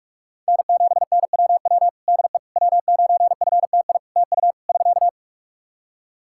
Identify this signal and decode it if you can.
Morse code, heard as a keyed tone, about 46 wpm, 700 Hz. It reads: N7NWW DE W9FTI TU <SK>
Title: Practice QSO - Exchange 4 - Ending - 45wpm